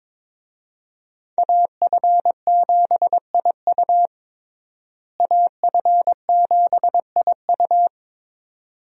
Morse code, heard as a keyed tone, about 22 wpm, 700 Hz. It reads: AF7IU AF7IU
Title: USA Call Signs - 5 Characters - Repeated 2x - 22wpm